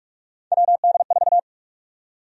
Morse code, heard as a keyed tone, about 44 wpm, 700 Hz. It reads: WD4
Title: USA Call Sign Prefixes - Repeated 2x - 45wpm